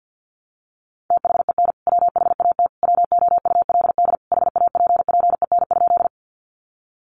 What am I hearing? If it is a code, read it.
Morse code, heard as a keyed tone, about 50 wpm, 700 Hz. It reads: THEN WHAT WOULD HAPPEN?
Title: Sentences from Top 400 Words 50wpm